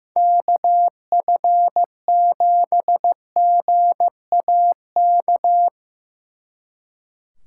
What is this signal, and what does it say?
Morse code, heard as a keyed tone, about 17 wpm, 700 Hz. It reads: KF7GAK
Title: Speed Racing - USA 2x3 Call Signs 15wpm